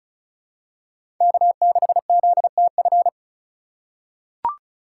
Morse code, heard as a keyed tone, about 35 wpm, 700 Hz. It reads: K6ZTF
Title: USA Call Signs - 5 Characters - 3x Character Spacing - 35wpm